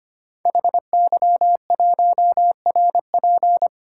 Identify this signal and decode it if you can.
Morse code, heard as a keyed tone, about 25 wpm, 700 Hz. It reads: HY1RP
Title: Code Groups - 5 Letter:Numbers - 25wpm